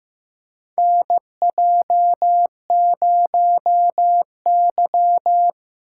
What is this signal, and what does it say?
Morse code, heard as a keyed tone, about 15 wpm, 700 Hz. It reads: NJ0Y
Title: USA Call Signs - 3x Character Spacing - 15wpm